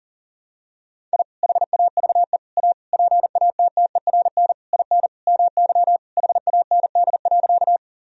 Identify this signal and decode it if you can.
Morse code, heard as a keyed tone, about 40 wpm, 700 Hz. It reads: I HAVE A PATTERN IN MY HAND.
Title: Sentences from Top 1000 Words - Review Part 2 - 40wpm